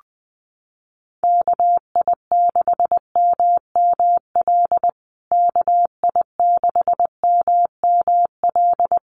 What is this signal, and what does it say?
Morse code, heard as a keyed tone, about 20 wpm, 700 Hz. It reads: KI6MML KI6MML
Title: USA Call Signs Repeated 3x 20wpm